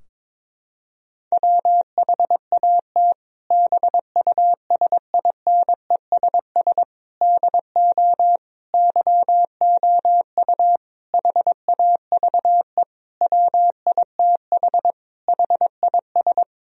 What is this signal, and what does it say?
Morse code, heard as a keyed tone, about 22 wpm, 700 Hz. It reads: WHAT BUSINESS DO YOU HAVE WITH HIS
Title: Sentences from Top 1000 Words - Review Part 3 - 22wpm